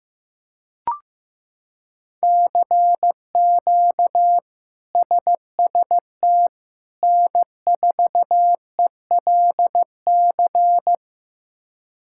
Morse code, heard as a keyed tone, about 16 wpm, 700 Hz. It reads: CQ SST N4ELC
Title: K1USN SST QSO Practice - 15wpm